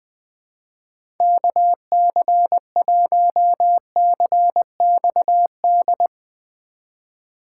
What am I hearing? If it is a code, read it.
Morse code, heard as a keyed tone, about 20 wpm, 700 Hz. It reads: KC1CXD
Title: USA Call Signs - 6 Characters - 3x Character Spacing - 20wpm